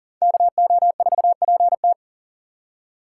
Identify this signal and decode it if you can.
Morse code, heard as a keyed tone, about 47 wpm, 700 Hz. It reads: KO4PT
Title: Speed Racing - USA Call Signs 40wpm